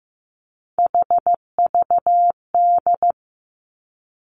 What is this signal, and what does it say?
Morse code, heard as a keyed tone, about 15 wpm, 700 Hz. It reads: HVD
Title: Three Challenging Letter-Number Combinations 15wpm